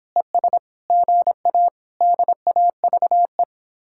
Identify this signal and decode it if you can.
Morse code, heard as a keyed tone, about 31 wpm, 700 Hz. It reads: ES GA DAVE